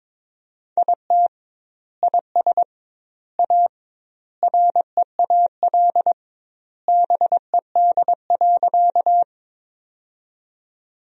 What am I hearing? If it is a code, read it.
Morse code, heard as a keyed tone, about 22 wpm, 700 Hz. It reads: IT IS A REAL BED.
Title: 2x Word-Spacing - Sentences from Top 500 Words - 22wpm